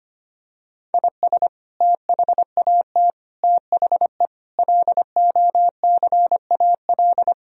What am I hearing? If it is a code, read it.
Morse code, heard as a keyed tone, about 25 wpm, 700 Hz. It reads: IS THAT THE LOCAL
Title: Sentences from Top 1000 Words - Review Part 3 - 25wpm